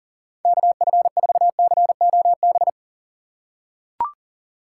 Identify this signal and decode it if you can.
Morse code, heard as a keyed tone, about 40 wpm, 700 Hz. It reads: KF4COB
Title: USA Call Signs - 6 Characters - 2x Character Spacing - 40wpm